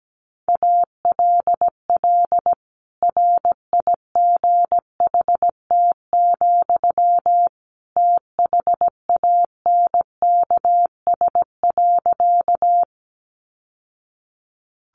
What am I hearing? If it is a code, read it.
Morse code, heard as a keyed tone, about 17 wpm, 700 Hz. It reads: ALL RIGHT, THANKS.
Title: Sentences from Top 600 Words 17wpm